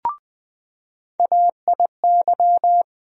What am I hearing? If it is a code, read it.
Morse code, heard as a keyed tone, about 21 wpm, 700 Hz. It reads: AIY
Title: USA Call Sign Postfixes - 3 Letters 20wpm